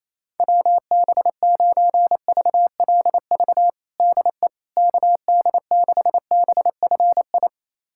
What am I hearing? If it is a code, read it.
Morse code, heard as a keyed tone, about 28 wpm, 700 Hz. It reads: WB9VLV DE KD6BFI
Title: USA Callsign DE Callsign 28wpm